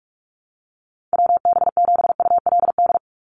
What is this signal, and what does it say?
Morse code, heard as a keyed tone, about 44 wpm, 690 Hz. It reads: WB7ULD
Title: NA SOTA Chaser Callsigns 45wpm